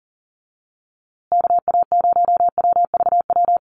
Morse code, heard as a keyed tone, about 40 wpm, 700 Hz. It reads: KA0WVW
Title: USA Call Signs Repeated 3x 40wpm